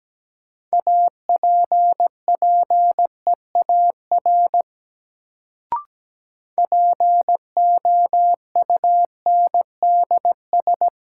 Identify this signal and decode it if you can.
Morse code, heard as a keyed tone, about 17 wpm, 700 Hz. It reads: APPEAR POUNDS
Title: Sentences from Top 1000 Words - Review Part 2 - New Word Drill - 17wpm